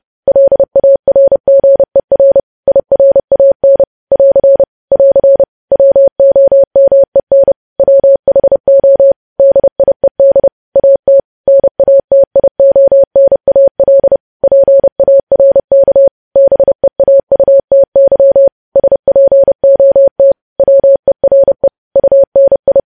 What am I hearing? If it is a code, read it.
Morse code, heard as a keyed tone, about 30 wpm, 550 Hz. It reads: LARGER IRAN <AR> <AR> WOMEN WHO DIED AT NATIONAL PARK BEAUTY SPOT WERE UNI
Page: news headlines in morse code at 30 wpm